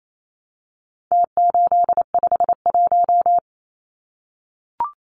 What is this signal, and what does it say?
Morse code, heard as a keyed tone, about 28 wpm, 700 Hz. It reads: T851
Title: Numbers - Contest Style - 4-Digit 28wpm